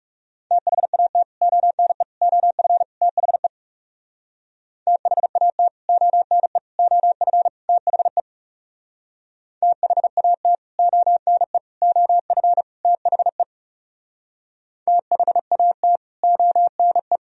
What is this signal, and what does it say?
Morse code, heard as a keyed tone, about 37 wpm, 700 Hz. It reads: THAT ONE OF THE THAT ONE OF THE THAT ONE OF THE THAT ONE
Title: Speed-Racing - Sets of 4 Words 30wpm